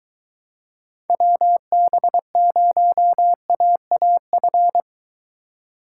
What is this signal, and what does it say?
Morse code, heard as a keyed tone, about 21 wpm, 700 Hz. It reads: WB0AAF